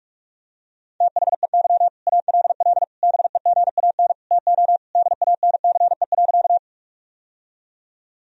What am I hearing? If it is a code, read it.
Morse code, heard as a keyed tone, about 41 wpm, 700 Hz. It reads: THEY ALL BEGAN TO DANCE.
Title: Sentences from Top 1000 Words - Review Part 4 - 45wpm